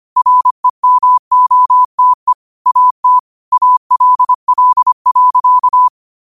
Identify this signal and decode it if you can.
Morse code, heard as a keyed tone, about 25 wpm, 1000 Hz. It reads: REMOTE AT ALL.
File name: qotdfiles / 25wpm